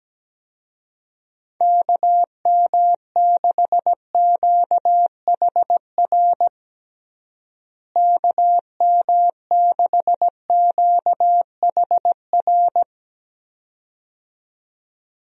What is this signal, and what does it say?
Morse code, heard as a keyed tone, about 17 wpm, 700 Hz. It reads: KM6QHR KM6QHR
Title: USA Call Signs - 6 Characters - Repeated 2x - 17wpm